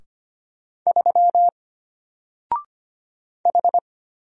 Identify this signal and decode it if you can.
Morse code, heard as a keyed tone, about 25 wpm, 700 Hz. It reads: 3 H
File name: Single-Letter-Number-25wpm